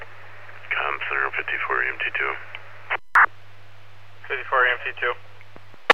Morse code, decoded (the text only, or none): none